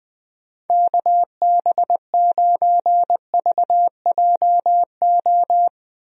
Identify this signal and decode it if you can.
Morse code, heard as a keyed tone, about 21 wpm, 700 Hz. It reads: KB9VJO
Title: USA Call Signs - 6 Characters - Repeated 3x - 20wpm